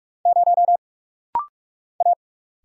Morse code, heard as a keyed tone, about 46 wpm, 700 Hz. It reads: 0 A
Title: Single Letter-Number - Rapid-Fire 45wpm